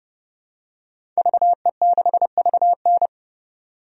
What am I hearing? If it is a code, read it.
Morse code, heard as a keyed tone, about 30 wpm, 700 Hz. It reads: VE6VN
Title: DXCC-100 Call Signs - 5 Characters - 30wpm